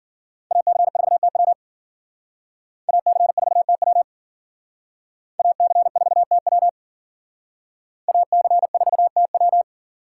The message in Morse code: AC4TW AC4TW AC4TW AC4TW

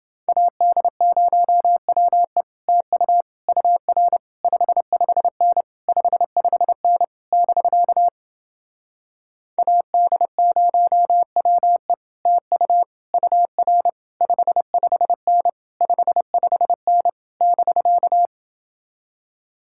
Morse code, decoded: AD0WE TU UR 55N 55N <BK> AD0WE TU UR 55N 55N <BK>

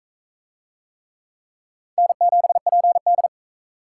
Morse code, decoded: N7PD

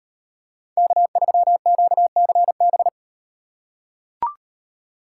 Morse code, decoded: K3QCB

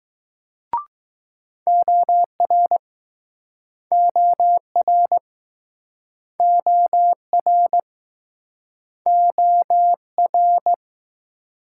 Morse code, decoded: OR OR OR OR

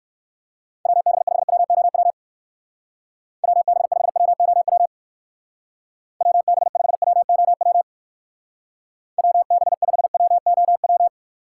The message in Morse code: WB5WOW WB5WOW WB5WOW WB5WOW

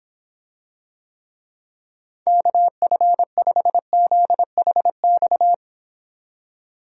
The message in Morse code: KF5ZHX

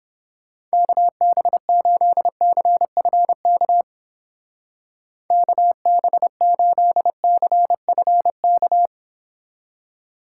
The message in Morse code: KB8CFK KB8CFK

